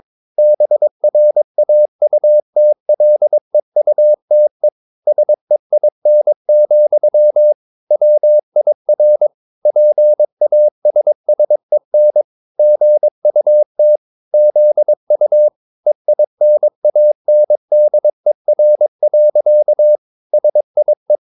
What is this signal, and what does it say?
Morse code, heard as a keyed tone, about 22 wpm, 600 Hz. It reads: BRAUTLEUTE SEIN, WIR PASSEN GUT ZU EINANDER. SIE